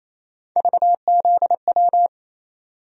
Morse code, heard as a keyed tone, about 28 wpm, 700 Hz. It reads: VZW
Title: USA Call Sign Postfixes - 3 Letters 28wpm